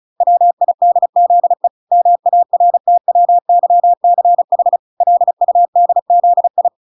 Morse code, heard as a keyed tone, about 34 wpm, 700 Hz. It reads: WIDZE MARTWYCH LUDZI